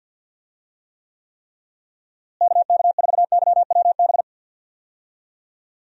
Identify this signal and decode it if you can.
Morse code, heard as a keyed tone, about 52 wpm, 700 Hz. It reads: KK4YWB
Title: USA Call Signs - 6 Characters - Repeated 3x - 50wpm